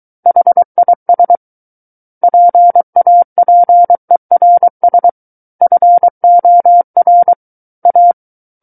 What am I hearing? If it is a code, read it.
Morse code, heard as a keyed tone, about 23 wpm, 700 Hz. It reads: HIS PAPERS FOR A
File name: Book One / 14-Chapter 14 - 23 WPM